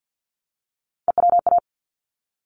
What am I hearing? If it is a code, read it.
Morse code, heard as a keyed tone, about 50 wpm, 700 Hz. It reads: EWA
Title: ARRL Sections 50wpm